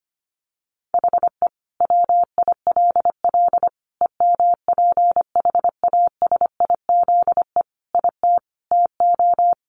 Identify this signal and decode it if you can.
Morse code, heard as a keyed tone, about 25 wpm, 700 Hz. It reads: HE WILL EMPHASIZE IT TO
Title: Sentences from Top 1500 Words 25wpm